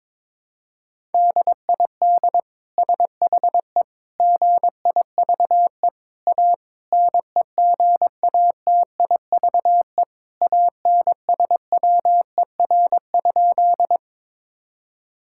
Morse code, heard as a keyed tone, about 21 wpm, 700 Hz. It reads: DID SHE GIVE A NEGATIVE ANSWER?
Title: Sentences from Top 1000 Words - Review Part 6 - 22wpm